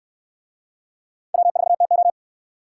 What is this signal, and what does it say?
Morse code, heard as a keyed tone, about 69 wpm, 700 Hz. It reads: W4TO